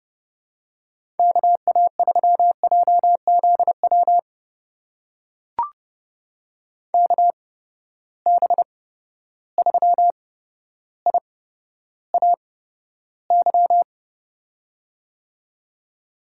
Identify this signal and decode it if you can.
Morse code, heard as a keyed tone, about 30 wpm, spaced as for 13 wpm, 700 Hz. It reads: KA3JZW KB3IAY
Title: USA Call Signs - 6 Characters - 8x Character Spacing - 30wpm